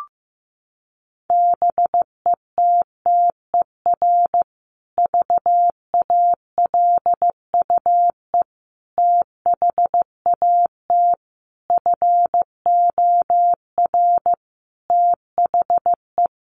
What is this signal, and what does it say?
Morse code, heard as a keyed tone, about 14 wpm, 700 Hz. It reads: BETTER VALUE THAT FOR THE
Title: Sentences from Top 600 Words 15wpm